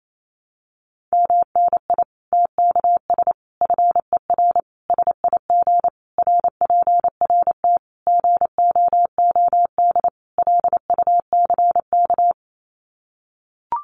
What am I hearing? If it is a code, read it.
Morse code, heard as a keyed tone, about 28 wpm, 700 Hz. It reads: MNI TKS FER SIG RPRT GOOD LUCK
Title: Common QSO Phrases 28wpm